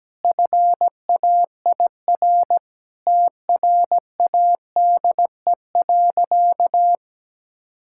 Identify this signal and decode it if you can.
Morse code, heard as a keyed tone, about 17 wpm, 700 Hz. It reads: FAIR TRADE.
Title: Sentences from Top 1600 Words 17wpm